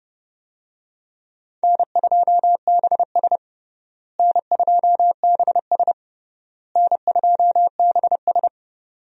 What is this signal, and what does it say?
Morse code, heard as a keyed tone, about 32 wpm, 700 Hz. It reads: N2BS N2BS N2BS
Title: USA Call Signs - 4 Characters - Repeated 3x - 30wpm